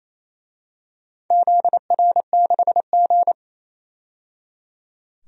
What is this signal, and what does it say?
Morse code, heard as a keyed tone, about 28 wpm, 700 Hz. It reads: ZR6G
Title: DXCC-100 Call Signs - 4 Characters - 28wpm